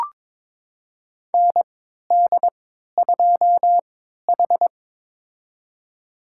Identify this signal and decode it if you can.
Morse code, heard as a keyed tone, about 21 wpm, spaced as for 12 wpm, 700 Hz. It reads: ND2H
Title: USA Call Signs - 3x Character Spacing - 22wpm